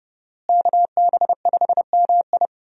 Morse code, heard as a keyed tone, about 30 wpm, 700 Hz. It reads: KB5MI
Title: USA Call Signs - 5 Characters - Repeated 3x - 30wpm